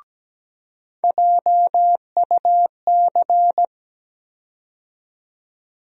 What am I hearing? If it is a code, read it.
Morse code, heard as a keyed tone, about 17 wpm, 700 Hz. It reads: JUC
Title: USA Call Sign Postfixes - 3 Letters 17wpm